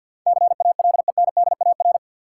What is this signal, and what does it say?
Morse code, heard as a keyed tone, about 53 wpm, 700 Hz. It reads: CALENDAR
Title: Sentences from Top 1600 Words - New Word Drill Alt - 50wpm